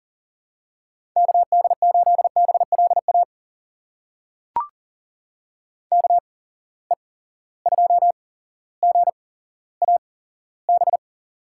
Morse code, heard as a keyed tone, about 40 wpm, spaced as for 17 wpm, 700 Hz. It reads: KD8BLA KE2GAB
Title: USA Call Signs - 6 Characters - 8x Character Spacing - 40wpm